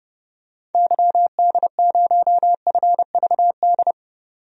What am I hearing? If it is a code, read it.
Morse code, heard as a keyed tone, about 31 wpm, 690 Hz. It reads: YD0FVD
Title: DXCC-100 Call Signs - 6 Characters - 30wpm